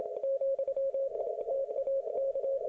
Morse code, heard as a keyed tone, about 26 wpm, 540 Hz. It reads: , TAK